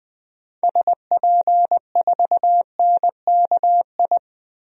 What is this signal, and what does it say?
Morse code, heard as a keyed tone, about 20 wpm, 700 Hz. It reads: SP4NKI
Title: DXCC-100 Call Signs - 6 Characters - Repeated 2x - 20wpm